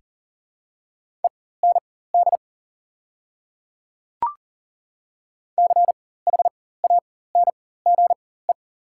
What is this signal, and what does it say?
Morse code, heard as a keyed tone, about 40 wpm, spaced as for 19 wpm, 700 Hz. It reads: END CHANGE
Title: 4x Character Spacing - Top 200 Words - 40wpm